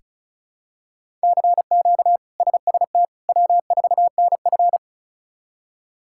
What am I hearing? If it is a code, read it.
Morse code, heard as a keyed tone, about 36 wpm, 700 Hz. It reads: CQ SST W4NF
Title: K1USN SST QSO Practice - 35wpm